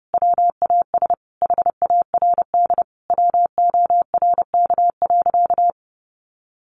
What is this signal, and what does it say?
Morse code, heard as a keyed tone, about 30 wpm, 700 Hz. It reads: WAS HARD WORK.
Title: Sentences from Top 600 Words 30wpm